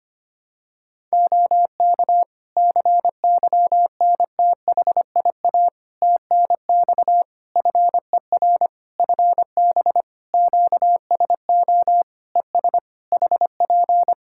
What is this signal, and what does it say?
Morse code, heard as a keyed tone, about 25 wpm, 700 Hz. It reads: OK CYNTHIA TNX FER FB QSO ES HP
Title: K1USN SST QSO Practice 25wpm